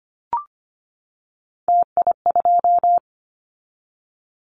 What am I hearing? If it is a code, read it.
Morse code, heard as a keyed tone, about 25 wpm, 700 Hz. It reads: TI2
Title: Callsign Prefixes for Top 100 DXCC - 25wpm